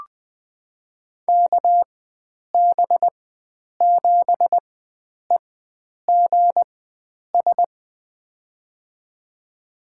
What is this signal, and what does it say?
Morse code, heard as a keyed tone, about 20 wpm, spaced as for 9 wpm, 700 Hz. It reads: KB7EGS
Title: USA Call Signs - 6 Characters - 4x Character Spacing - 20wpm